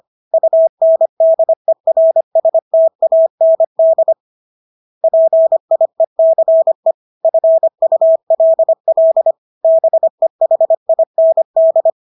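Morse code, heard as a keyed tone, about 25 wpm, 650 Hz. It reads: UNDERSTAND PIECE FULL BEHIND
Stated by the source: Morse practice of English words at 25 wpm from CW med Gnister.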